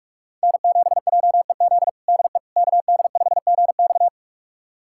Morse code, heard as a keyed tone, about 47 wpm, 700 Hz. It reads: N7JEZ DE KD5GX
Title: USA Callsign DE Callsign - 5 Characters - 45wpm